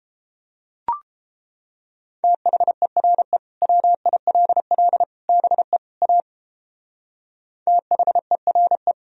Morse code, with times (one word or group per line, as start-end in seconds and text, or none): 2.2-3.4 THERE
3.6-5.0 WILL
5.3-5.8 BE
6.0-6.2 A
7.7-8.9 THERE